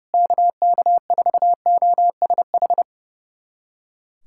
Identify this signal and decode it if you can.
Morse code, heard as a keyed tone, about 30 wpm, 700 Hz. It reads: KK4OSH
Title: USA Callsign DE Callsign 30wpm